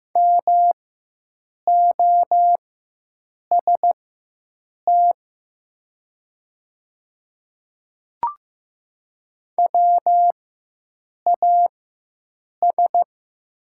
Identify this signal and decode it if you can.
Morse code, heard as a keyed tone, about 15 wpm, spaced as for 7 wpm, 700 Hz. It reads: MOST WAS
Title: 4x Character Spacing - Top 100 Words - 15wpm